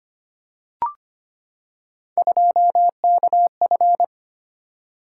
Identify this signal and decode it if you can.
Morse code, heard as a keyed tone, about 25 wpm, 700 Hz. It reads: 2KF
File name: Three Letter-Number Combinations - 25wpm